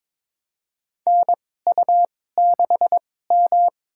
Morse code, heard as a keyed tone, about 22 wpm, spaced as for 16 wpm, 700 Hz. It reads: NU6M
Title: USA Call Signs - 2x Character Spacing - 22wpm